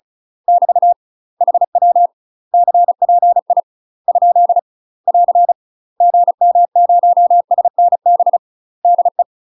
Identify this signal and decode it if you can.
Morse code, heard as a keyed tone, about 36 wpm, spaced as for 31 wpm, 700 Hz. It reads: <BT> HW CPI? <AR> GM0SNB DE
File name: UK MASTER QSOs TRAINING / 35WPM / UK MASTER QSOs 0036